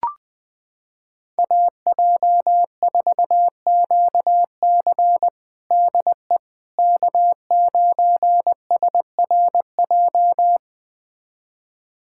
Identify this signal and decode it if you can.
Morse code, heard as a keyed tone, about 20 wpm, 700 Hz. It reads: AJ4QC DE K9SRJ
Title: USA Callsign DE Callsign - 5 Characters - 20wpm